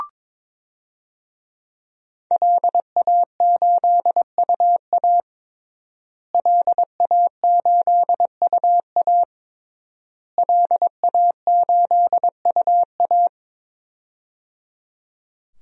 Morse code, heard as a keyed tone, about 22 wpm, 700 Hz. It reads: LA8UA LA8UA LA8UA
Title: DXCC-100 Call Signs - 5 Characters - Repeated 3x - 22wpm